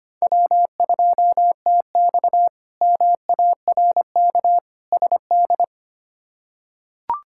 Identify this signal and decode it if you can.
Morse code, heard as a keyed tone, about 25 wpm, 700 Hz. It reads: W2TX MARK SD
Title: K1USN SST QSO Practice - 25wpm